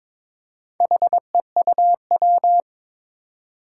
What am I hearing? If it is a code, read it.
Morse code, heard as a keyed tone, about 22 wpm, 700 Hz. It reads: HEUW
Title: Four Challenging Letter-Number Combinations 22wpm